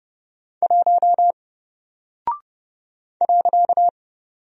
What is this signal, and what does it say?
Morse code, heard as a keyed tone, about 30 wpm, 700 Hz. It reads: 1 .
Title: Single Character (Letter:Number:Punctuation:Prosign) Alt 30wpm